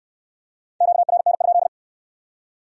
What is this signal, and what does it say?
Morse code, heard as a keyed tone, about 68 wpm, 700 Hz. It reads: QRN?